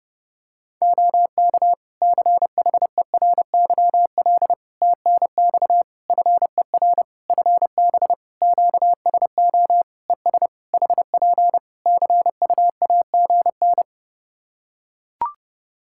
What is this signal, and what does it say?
Morse code, heard as a keyed tone, about 30 wpm, 700 Hz. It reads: OK CHERYL TNX FER FB QSO ES HP CUAGN
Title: Practice QSO - Exchange 4 - Ending - 30wpm